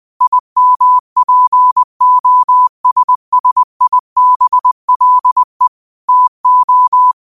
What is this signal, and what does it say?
Morse code, heard as a keyed tone, about 20 wpm, 1000 Hz. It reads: IMPOSSIBLE TO